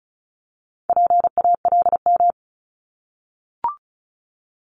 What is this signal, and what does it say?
Morse code, heard as a keyed tone, about 36 wpm, 700 Hz. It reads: PALM
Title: Multi-letter Phonemes Repeated 3x 35wpm